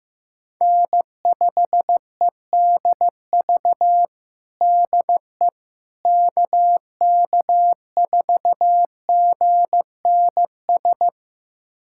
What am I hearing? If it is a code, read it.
Morse code, heard as a keyed tone, about 15 wpm, 700 Hz. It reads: N5EDV DE KK4GNS
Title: USA Callsign DE Callsign 15wpm